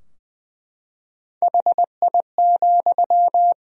Morse code, heard as a keyed tone, about 21 wpm, 700 Hz. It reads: HI,
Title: Sentences from Top 1000 Words - Review Part 3 - 20wpm